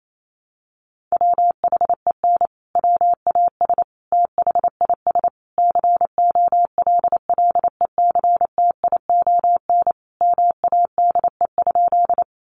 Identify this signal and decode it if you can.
Morse code, heard as a keyed tone, about 28 wpm, 700 Hz. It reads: WHEN WAS THIS COLLECTION MADE?
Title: Sentences from Top 1200 Words 28wpm